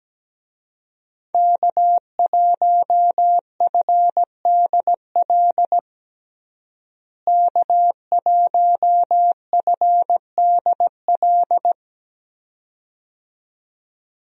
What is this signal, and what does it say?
Morse code, heard as a keyed tone, about 17 wpm, 700 Hz. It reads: K1FDL K1FDL
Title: USA Call Signs - 5 Characters - Repeated 2x - 17wpm